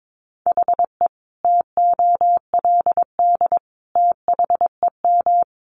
Morse code, heard as a keyed tone, about 22 wpm, 700 Hz. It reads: HE TOLD THEM
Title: Sentences from Top 700 Words 22wpm